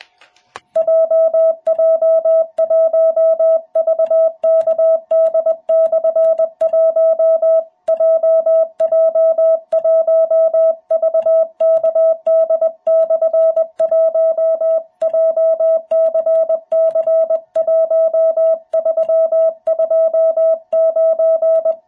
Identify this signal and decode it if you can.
Morse code, heard as a keyed tone, about 21 wpm, 660 Hz. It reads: JJ1VKD/1 JJ1VKD/1 JCC1329
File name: jh8cbh / 2021cw